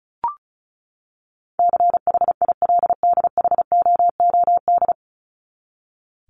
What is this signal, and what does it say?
Morse code, heard as a keyed tone, about 34 wpm, 700 Hz. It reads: CHILDHOOD
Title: Top 1751-2000 Words 35wpm